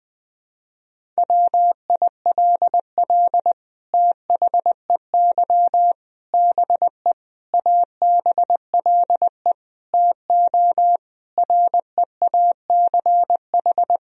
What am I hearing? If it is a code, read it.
Morse code, heard as a keyed tone, about 20 wpm, 700 Hz. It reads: WILL THEY BE ABLE TO REACH
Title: Sentences from Top 1000 Words - Review Part 2 - 20wpm